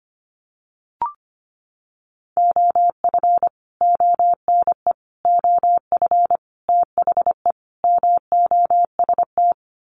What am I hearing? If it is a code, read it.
Morse code, heard as a keyed tone, about 25 wpm, 700 Hz. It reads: OF ONE OF THE MOST
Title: Sets of 5 Words Encore2 25wpm